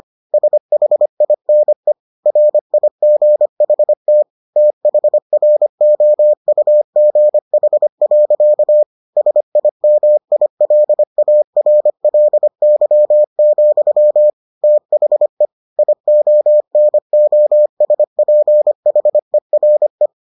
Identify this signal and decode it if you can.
Morse code, heard as a keyed tone, about 25 wpm, 600 Hz. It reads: SHINE RIGHT THROUGH. SIMILARLY, THE IONOSPHERE